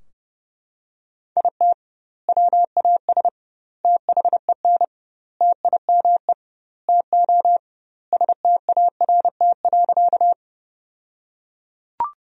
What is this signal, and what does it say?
Morse code, heard as a keyed tone, about 30 wpm, 700 Hz. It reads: IT WAS THEN TIME TO START.
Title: 2x Word-Spacing - Sentences from Top 300 Words - 30wpm